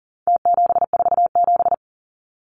Morse code, heard as a keyed tone, about 40 wpm, 700 Hz. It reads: T747
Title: Numbers - Contest Style - 4-Digit 40wpm